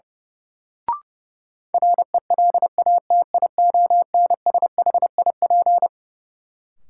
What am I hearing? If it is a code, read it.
Morse code, heard as a keyed tone, about 32 wpm, 700 Hz. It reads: RELATIONSHIP
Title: Sentences from Top 1100 Words - New Word Drill Alt - 30wpm